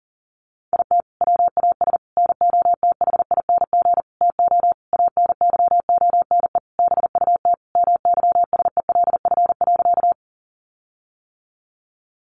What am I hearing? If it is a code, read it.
Morse code, heard as a keyed tone, about 40 wpm, 700 Hz. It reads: IT WAS NOTHING TO ANYONE BUT MYSELF.
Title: Sentences from Top 1300 Words 40wpm